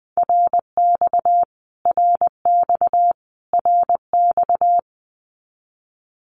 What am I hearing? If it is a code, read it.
Morse code, heard as a keyed tone, about 20 wpm, 700 Hz. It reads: RX RX RX
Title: Abbreviations Repeated 3x - 20wpm